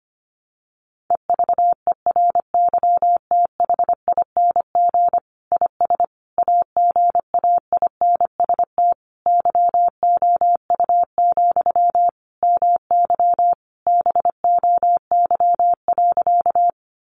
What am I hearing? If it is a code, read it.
Morse code, heard as a keyed tone, about 25 wpm, 700 Hz. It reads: EVERYTHING IS AGAINST YOU, MY BOY.
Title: Sentences from Top 700 Words 25wpm